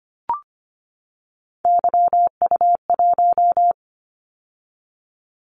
Morse code, heard as a keyed tone, about 25 wpm, 700 Hz. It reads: YU1
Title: Callsign Prefixes for Top 25 DXCC - 25wpm